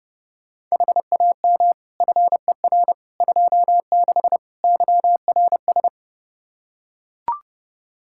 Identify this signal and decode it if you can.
Morse code, heard as a keyed tone, about 30 wpm, 700 Hz. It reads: HAM FER 26 YRS
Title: Practice QSO - Exchange 3 - More Info - 30wpm